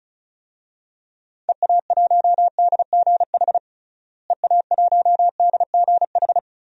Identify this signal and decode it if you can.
Morse code, heard as a keyed tone, about 36 wpm, 700 Hz. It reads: EA1DGH EA1DGH
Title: DXCC-100 Call Signs - 6 Characters - Repeated 3x - 35wpm